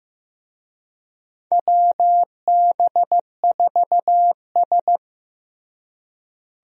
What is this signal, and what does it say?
Morse code, heard as a keyed tone, about 15 wpm, 700 Hz. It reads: WB4S
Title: USA Call Signs - 4 Characters - Repeated 2x - 15wpm